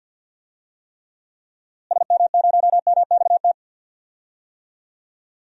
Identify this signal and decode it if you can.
Morse code, heard as a keyed tone, about 46 wpm, 700 Hz. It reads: SM0MXT
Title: DXCC-100 Call Signs - 6 Characters - Repeated 3x - 50wpm